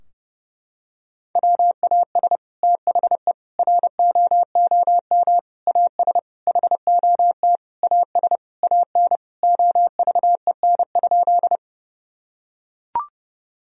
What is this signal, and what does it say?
Morse code, heard as a keyed tone, about 32 wpm, 700 Hz. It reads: WAS THE ROOM AS HOT AS AN OVEN?
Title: Sentences from Top 1000 Words - Review Part 4 - 30wpm